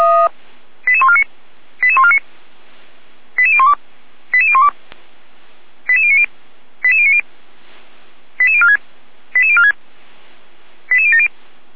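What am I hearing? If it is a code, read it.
Five-tone selective call (ZVEI), heard as quick runs of five tones, tones of about 70 ms. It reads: 88149 88149 88012 88012 88090 88090 88046 88046 88070
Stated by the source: ZVEI five-tone selective call received by SDR, from audio.f85360.feuerwehr-orlamuende.org fire brigade server.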